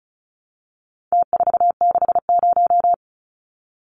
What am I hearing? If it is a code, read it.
Morse code, heard as a keyed tone, about 34 wpm, 700 Hz. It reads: T460
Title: Numbers - Contest Style - 4-Digit 35wpm